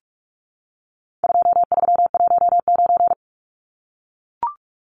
Morse code, heard as a keyed tone, about 44 wpm, 700 Hz. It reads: 2319
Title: Numbers - 4-Digit 45wpm